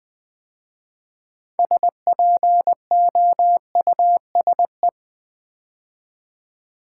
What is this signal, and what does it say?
Morse code, heard as a keyed tone, about 20 wpm, 700 Hz. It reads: SPOUSE
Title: 6 Letter Words - First 3 Letters Given - 20wpm